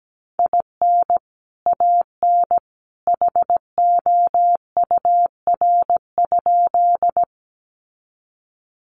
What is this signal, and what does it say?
Morse code, heard as a keyed tone, about 17 wpm, 700 Hz. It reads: IN AN HOUR?
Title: Sentences from Top 400 Words 17wpm